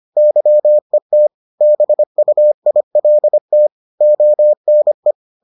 Morse code, heard as a keyed tone, about 25 wpm, 600 Hz. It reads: YET BUILT ONE